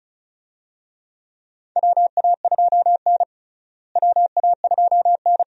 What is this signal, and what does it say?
Morse code, heard as a keyed tone, about 36 wpm, 700 Hz. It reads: WA2N WA2N
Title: USA Call Signs - 4 Characters - Repeated 2x - 35wpm